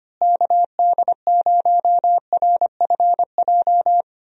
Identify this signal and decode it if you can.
Morse code, heard as a keyed tone, about 25 wpm, 700 Hz. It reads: KD0RFJ
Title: USA Call Signs - 6 Characters - Repeated 3x - 25wpm